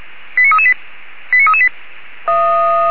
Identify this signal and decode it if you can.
Five-tone selective call (ZVEI), heard as quick runs of five tones, tones of about 70 ms. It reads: 89338 89338